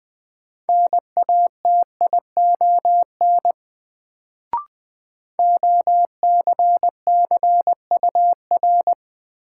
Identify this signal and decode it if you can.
Morse code, heard as a keyed tone, about 20 wpm, 700 Hz. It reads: NATION OCCUR
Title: Sentences from Top 900 Words - New Word Drill - 20wpm